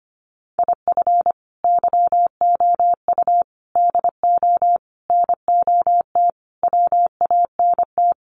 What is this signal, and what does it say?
Morse code, heard as a keyed tone, about 25 wpm, 700 Hz. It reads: IF YOU DO NOT WANT
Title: Sets of 5 Words 25wpm